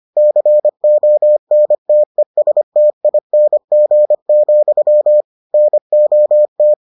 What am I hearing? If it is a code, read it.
Morse code, heard as a keyed tone, about 25 wpm, 600 Hz. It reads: CONTESTING, NOT